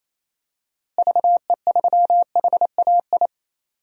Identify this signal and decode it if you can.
Morse code, heard as a keyed tone, about 28 wpm, 700 Hz. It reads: VE3HAI
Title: DXCC-100 Call Signs - 6 Characters - Repeated 2x - 28wpm